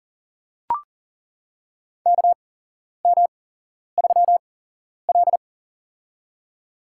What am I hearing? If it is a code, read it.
Morse code, heard as a keyed tone, about 39 wpm, spaced as for 11 wpm, 690 Hz. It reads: KM3L
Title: USA Call Signs - 4 Characters - 8x Character Spacing - 40wpm